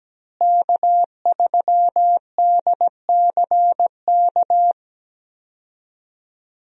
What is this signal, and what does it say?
Morse code, heard as a keyed tone, about 17 wpm, 700 Hz. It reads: K3DCK
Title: USA Call Signs - 5 Characters - Repeated 2x - 17wpm